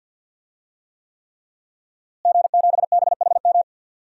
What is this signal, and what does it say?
Morse code, heard as a keyed tone, about 52 wpm, 700 Hz. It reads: G7BHM
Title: DXCC-100 Call Signs - 5 Characters - Repeated 3x - 50wpm